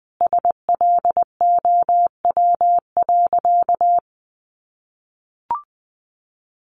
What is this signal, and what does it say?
Morse code, heard as a keyed tone, about 19 wpm, 700 Hz. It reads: SLOW.
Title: Sentences from Top 1400 Words 20wpm